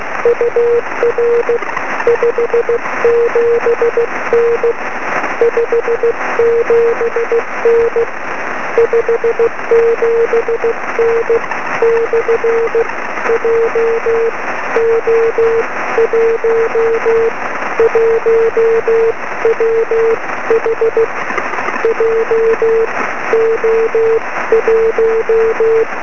Morse code, heard as a keyed tone, about 16 wpm, spaced as for 14 wpm, 470 Hz, with noise. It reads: UR 57N 57N 57N / JO1 1WH JO1
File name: sm7lcb / 20 23cm / 1813 pa0ghb 23cm